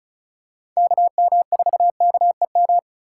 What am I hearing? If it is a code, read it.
Morse code, heard as a keyed tone, about 36 wpm, 700 Hz. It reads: KM4KEM
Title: USA Call Signs - 6 Characters - 4x Character Spacing - 35wpm